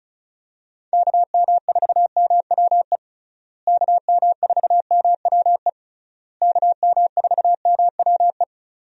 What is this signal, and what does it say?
Morse code, heard as a keyed tone, about 36 wpm, 700 Hz. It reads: KM4MWE KM4MWE KM4MWE
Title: USA Call Signs - 6 Characters - Repeated 3x - 35wpm